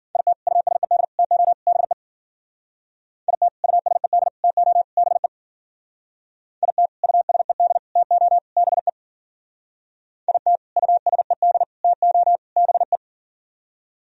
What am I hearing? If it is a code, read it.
Morse code, heard as a keyed tone, about 52 wpm, 700 Hz. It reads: IT USED TO BE IT USED TO BE IT USED TO BE IT USED TO BE